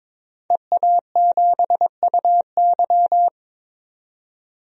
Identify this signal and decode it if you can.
Morse code, heard as a keyed tone, about 22 wpm, 700 Hz. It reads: EA7UY
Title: DXCC-100 Call Signs - 5 Characters - 22wpm